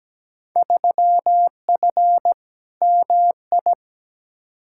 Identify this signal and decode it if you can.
Morse code, heard as a keyed tone, about 17 wpm, 700 Hz. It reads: 3F MI
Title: Field Day Exchanges - 17wpm